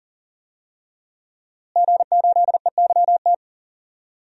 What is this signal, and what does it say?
Morse code, heard as a keyed tone, about 40 wpm, 700 Hz. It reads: G8EYT
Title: DXCC-100 Call Signs - 5 Characters - Repeated 3x - 40wpm